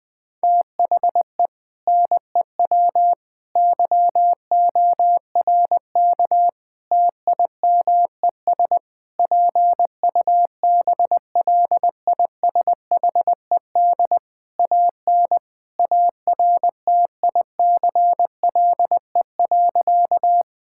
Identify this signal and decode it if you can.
Morse code, heard as a keyed tone, about 20 wpm, 700 Hz. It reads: THE NEW YORK TIMES PUBLISHED AN ARTICLE.
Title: Sentences with 4-Word Phrases - Spoken Beginning - 20wpm